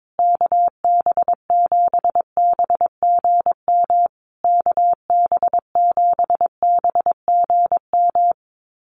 Morse code, heard as a keyed tone, about 22 wpm, 700 Hz. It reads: KB7BGM KB7BGM
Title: USA Call Signs Repeated 3x 22wpm